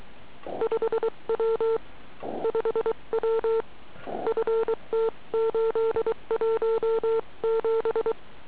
Morse code, heard as a keyed tone, about 23 wpm, 450 Hz. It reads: 5W 5W FT817